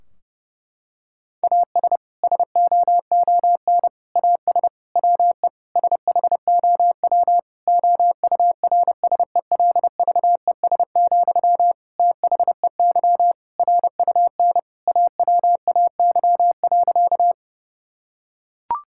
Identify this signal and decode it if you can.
Morse code, heard as a keyed tone, about 32 wpm, 700 Hz. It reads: AS SOON AS WE SHOW OURSELVES, THEY RUN AWAY.
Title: Sentences from Top 1600 Words 30wpm